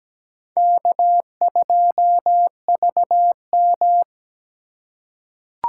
Morse code, heard as a keyed tone, about 17 wpm, 700 Hz. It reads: K2VM
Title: USA Call Signs - 4 Characters - 8x Character Spacing - 17wpm